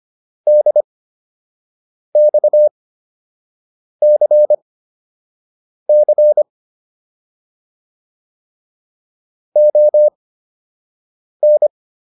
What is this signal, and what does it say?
Morse code, heard as a keyed tone, about 25 wpm, spaced as for 6 wpm, 600 Hz. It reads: DXCC ON